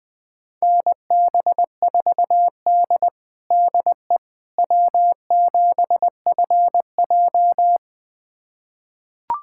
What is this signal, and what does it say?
Morse code, heard as a keyed tone, about 20 wpm, 700 Hz. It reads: NB4D DE W7FJ
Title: USA Callsign DE Callsign - 4 Characters - 20wpm